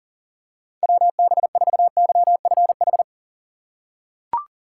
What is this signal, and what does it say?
Morse code, heard as a keyed tone, about 40 wpm, 700 Hz. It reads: WB4YFH
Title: USA Call Signs - 6 Characters - 4x Character Spacing - 40wpm